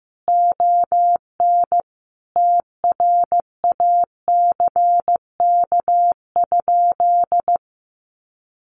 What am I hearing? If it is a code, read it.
Morse code, heard as a keyed tone, about 15 wpm, 700 Hz. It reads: ON TRACK?
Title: Sentences from Top 1200 Words 15wpm